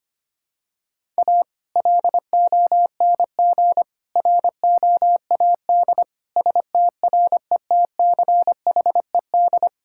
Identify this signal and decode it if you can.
Morse code, heard as a keyed tone, about 25 wpm, 700 Hz. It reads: A LONG ROAD STRETCHED
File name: practice / 25wpm / Sentences with 2-Word Phrases - 25wpm